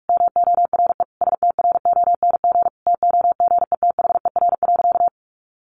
Morse code, heard as a keyed tone, about 46 wpm, 700 Hz. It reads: MORE STRONG TOGETHER.
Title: Sentences from Top 500 Words 45wpm